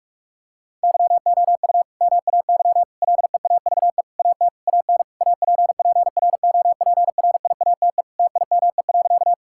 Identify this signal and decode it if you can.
Morse code, heard as a keyed tone, about 41 wpm, 700 Hz. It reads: YOU MAY LEAVE AT AN APPROPRIATE TIME.
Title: Sentences from Top 1000 Words - Review Part 5 - 45wpm